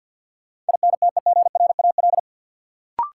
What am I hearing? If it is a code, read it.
Morse code, heard as a keyed tone, about 47 wpm, 700 Hz. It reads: INTEGRAL
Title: Sets of 5 Words - New Word Drill - 50wpm